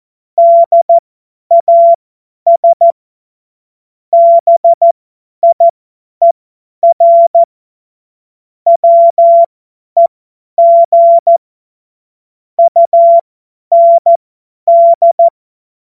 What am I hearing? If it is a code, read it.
Morse code, heard as a keyed tone, about 14 wpm, spaced as for 10 wpm, 680 Hz. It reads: DAS BIER WEG UND
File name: Morsekurs / Witze / Witz 27. Juli 2021, 50 F70